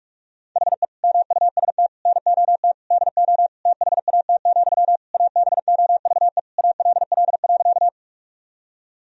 Morse code, heard as a keyed tone, about 46 wpm, 700 Hz. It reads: HE MUST NOT DO THAT, ABOVE ALL.
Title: Sentences from Top 1000 Words - Review Part 2 - 45wpm